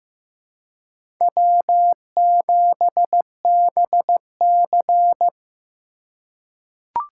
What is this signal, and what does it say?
Morse code, heard as a keyed tone, about 15 wpm, 700 Hz. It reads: W7BC
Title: USA Call Signs - 4 Characters - 4x Character Spacing - 15wpm